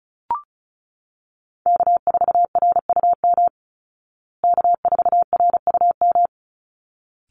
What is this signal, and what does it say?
Morse code, heard as a keyed tone, about 34 wpm, 700 Hz. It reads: K4RUM K4RUM
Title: Call Signs Repeated 3x - 35wpm